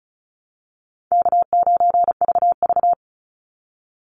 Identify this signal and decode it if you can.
Morse code, heard as a keyed tone, about 34 wpm, 700 Hz. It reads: K9VV
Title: Call Signs - 35wpm